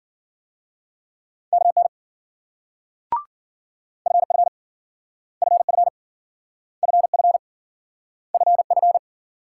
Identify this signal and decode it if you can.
Morse code, heard as a keyed tone, about 53 wpm, 700 Hz. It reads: KN FF FF FF FF